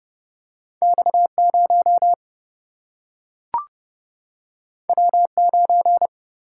Morse code, heard as a keyed tone, about 30 wpm, 700 Hz. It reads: X0 W9
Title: Single-Letter Single-Number 30wpm